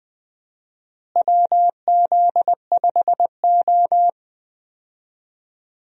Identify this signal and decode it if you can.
Morse code, heard as a keyed tone, about 20 wpm, 700 Hz. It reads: WZ5O
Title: USA Call Signs - 4 Characters - Repeated 3x - 20wpm